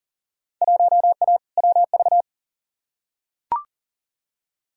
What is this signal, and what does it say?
Morse code, heard as a keyed tone, about 40 wpm, 700 Hz. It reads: 1A WV
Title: Field Day Exchanges Encore - 40wpm